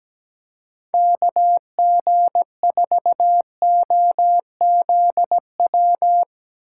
Morse code, heard as a keyed tone, about 17 wpm, 700 Hz. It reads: KG4OZW
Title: USA Call Signs - 6 Characters - 4x Character Spacing - 17wpm